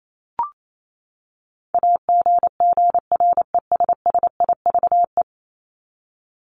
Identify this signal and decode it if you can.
Morse code, heard as a keyed tone, about 28 wpm, 700 Hz. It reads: AGGRESSIVE
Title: Top 1501-1750 Words 28wpm